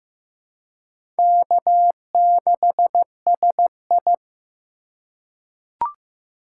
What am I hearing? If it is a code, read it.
Morse code, heard as a keyed tone, about 15 wpm, 700 Hz. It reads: K6SI
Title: USA Call Signs - 4 Characters - 4x Character Spacing - 15wpm